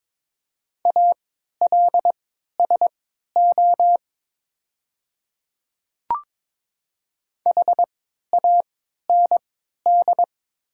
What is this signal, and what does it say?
Morse code, heard as a keyed tone, about 22 wpm, spaced as for 12 wpm, 700 Hz. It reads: ALSO HAND
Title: 3x Character Spacing - Top 300 Words - 22wpm